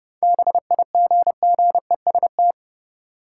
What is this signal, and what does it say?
Morse code, heard as a keyed tone, about 32 wpm, 700 Hz. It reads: BIGGEST
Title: Sets of 4 Words - New Word Drill - 30wpm